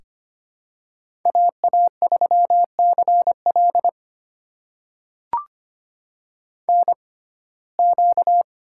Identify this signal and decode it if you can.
Morse code, heard as a keyed tone, about 25 wpm, spaced as for 18 wpm, 700 Hz. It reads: AA3CL NQ
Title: USA Call Signs - 6x Character Spacing - 25wpm